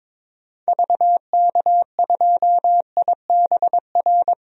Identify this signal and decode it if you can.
Morse code, heard as a keyed tone, about 22 wpm, 700 Hz. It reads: VK2IBR
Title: DXCC-100 Call Signs - 6 Characters - 22wpm